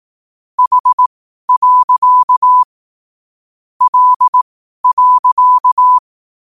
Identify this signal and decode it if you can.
Morse code, heard as a keyed tone, about 18 wpm, spaced as for 12 wpm, 1000 Hz. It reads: H. L.
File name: qotdfiles / 12wpm